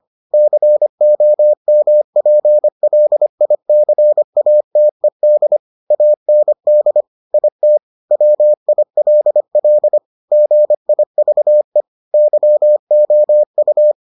Morse code, heard as a keyed tone, about 25 wpm, 600 Hz. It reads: COMPLICATED AND IT WILL GIVE YOU